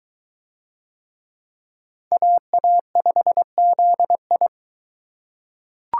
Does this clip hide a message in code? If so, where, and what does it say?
Yes: Morse code from 2.1 to 4.5 s, heard as a keyed tone, about 23 wpm, 700 Hz. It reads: AA5ZI